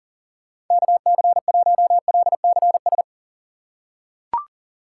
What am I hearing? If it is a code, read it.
Morse code, heard as a keyed tone, about 40 wpm, 700 Hz. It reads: KC1LCS
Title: USA Call Signs - 6 Characters - 6x Character Spacing - 40wpm